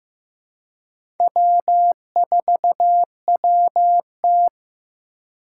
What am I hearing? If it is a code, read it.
Morse code, heard as a keyed tone, about 15 wpm, 700 Hz. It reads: W4WT
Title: USA Call Signs - 4 Characters - 2x Character Spacing - 15wpm